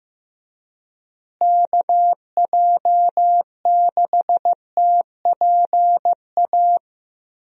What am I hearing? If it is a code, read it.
Morse code, heard as a keyed tone, about 15 wpm, 700 Hz. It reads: KJ6TPA
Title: USA Call Signs - 6 Characters - Repeated 2x - 15wpm